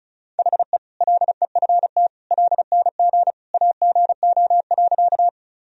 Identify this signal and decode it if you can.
Morse code, heard as a keyed tone, about 36 wpm, 700 Hz. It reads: HE LEFT LNG AGO.
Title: Sentences from Top 500 Words and Abbreviations 35wpm